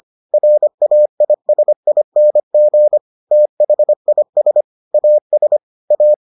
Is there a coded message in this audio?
Yes: Morse code from 0.3 to 6.1 s, heard as a keyed tone, about 25 wpm, 600 Hz. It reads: RAISING THIS AS A